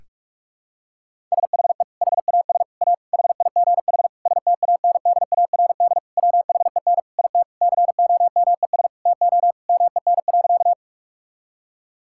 Morse code, heard as a keyed tone, about 46 wpm, 700 Hz. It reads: SHE HAS A HIGH STANDARD WHEN IT COMES TO MEN.
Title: Sentences from Top 1000 Words - Review Part 6 - 45wpm